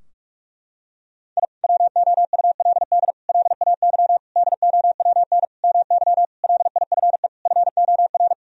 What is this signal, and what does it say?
Morse code, heard as a keyed tone, about 46 wpm, 700 Hz. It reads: I WOULD LAY DOWN MY LIFE FOR
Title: Sentences from Top 1000 Words - Review Part 2 - 45wpm